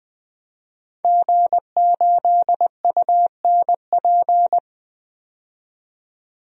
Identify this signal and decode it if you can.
Morse code, heard as a keyed tone, about 20 wpm, 700 Hz. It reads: G8UNP
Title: DXCC-100 Call Signs - 5 Characters - 20wpm